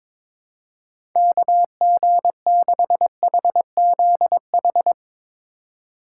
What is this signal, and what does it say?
Morse code, heard as a keyed tone, about 22 wpm, 700 Hz. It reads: KG6HZH
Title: USA Call Signs - 6 Characters - 3x Character Spacing - 22wpm